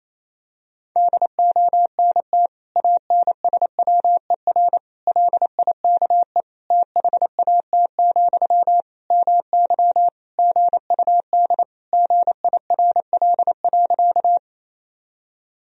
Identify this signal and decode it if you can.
Morse code, heard as a keyed tone, about 27 wpm, 700 Hz. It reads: DONT ANSWER LIKE THAT, MY GUD GIRL.
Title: Sentences from Top 400 Words and Abbreviations 28wpm